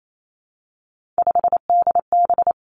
Morse code, heard as a keyed tone, about 28 wpm, 700 Hz. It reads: 5DB
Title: Three Challenging Letter-Number Combinations 28wpm